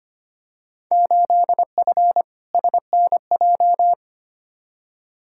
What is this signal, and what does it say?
Morse code, heard as a keyed tone, about 25 wpm, 700 Hz. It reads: 8F SNJ
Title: Field Day Exchanges Encore - 25wpm